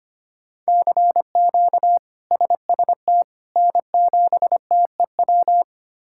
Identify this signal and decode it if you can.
Morse code, heard as a keyed tone, about 25 wpm, 700 Hz. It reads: CQ SST N7TEW
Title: K1USN SST QSO Practice - 25wpm